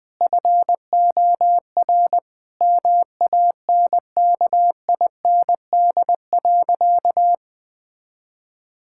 Morse code, heard as a keyed tone, about 21 wpm, 700 Hz. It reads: FOR MANKIND.
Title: Sentences with 9-Word Phrases - Spoken Beginning - 20wpm